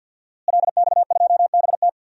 Morse code, heard as a keyed tone, about 47 wpm, 700 Hz. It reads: LX2BT
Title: DXCC-100 Call Signs - 5 Characters - Repeated 2x - 50wpm